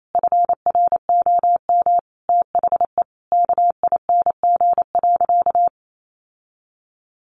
Morse code, heard as a keyed tone, about 28 wpm, 700 Hz. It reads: FROM THE KING.
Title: Sentences from Top 1400 Words 28wpm